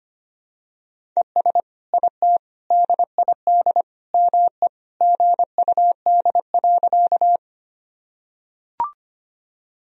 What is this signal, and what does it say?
Morse code, heard as a keyed tone, about 25 wpm, 700 Hz. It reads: ES IT DID ME GUD.
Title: Sentences from Top 200 Words and Abbreviations 25wpm